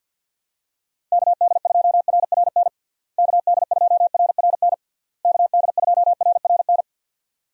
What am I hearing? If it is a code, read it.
Morse code, heard as a keyed tone, about 52 wpm, 700 Hz. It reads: KD2RRN KD2RRN KD2RRN
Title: USA Call Signs - 6 Characters - Repeated 3x - 50wpm